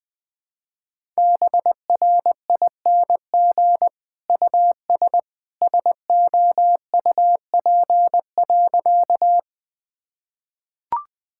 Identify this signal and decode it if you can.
Morse code, heard as a keyed tone, about 19 wpm, 700 Hz. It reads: BRING US SOUP.
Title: Sentences from Top 1000 Words - Review Part 6 - 20wpm